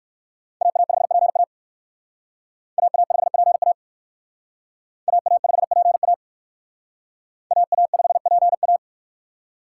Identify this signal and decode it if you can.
Morse code, heard as a keyed tone, about 58 wpm, 700 Hz. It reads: AA5PA AA5PA AA5PA AA5PA